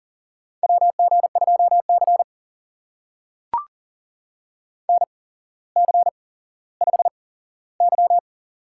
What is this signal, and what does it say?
Morse code, heard as a keyed tone, about 39 wpm, spaced as for 17 wpm, 700 Hz. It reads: WG2C NC5Y
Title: USA Call Signs - 4 Characters - 8x Character Spacing - 40wpm